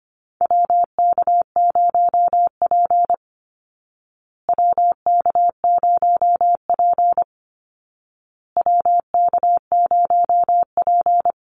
Morse code, heard as a keyed tone, about 25 wpm, 700 Hz. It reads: WK0P WK0P WK0P
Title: Call Signs Repeated 2x - 25wpm